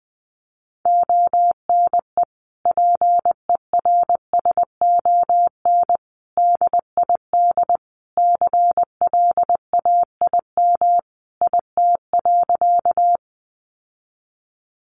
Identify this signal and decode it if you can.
Morse code, heard as a keyed tone, about 19 wpm, 700 Hz. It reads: ONE PERSON DID CLAIM IT.
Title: Sentences from Top 1500 Words 20wpm